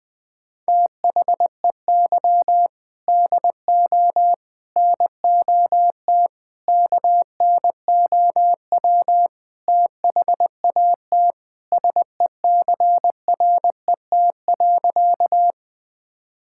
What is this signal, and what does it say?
Morse code, heard as a keyed tone, about 20 wpm, 700 Hz. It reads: THEY DO NOT KNOW THAT SECRET.
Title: Sentences with 4-Word Phrases - Spoken Beginning - 20wpm